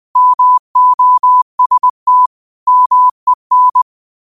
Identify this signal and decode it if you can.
Morse code, heard as a keyed tone, about 20 wpm, 1000 Hz. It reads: MOST MEN
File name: qotdfiles / 20wpm